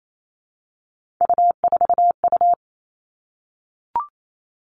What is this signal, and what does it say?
Morse code, heard as a keyed tone, about 28 wpm, 700 Hz. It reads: U4U
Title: Three Challenging Letter-Number Combinations 28wpm